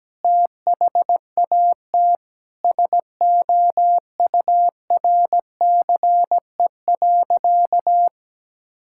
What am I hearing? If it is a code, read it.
Morse code, heard as a keyed tone, about 17 wpm, 700 Hz. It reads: THAT SOURCE.
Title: Sentences from Top 1000 Words - Review Part 5 - 17wpm